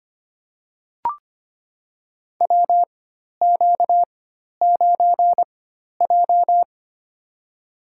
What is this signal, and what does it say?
Morse code, heard as a keyed tone, about 25 wpm, spaced as for 12 wpm, 700 Hz. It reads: WQ9J
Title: USA Call Signs - 4 Characters - 4x Character Spacing - 25wpm